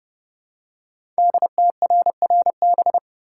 Morse code, heard as a keyed tone, about 30 wpm, 700 Hz. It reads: DTRRB
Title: Code Groups - 5 Letters - 30wpm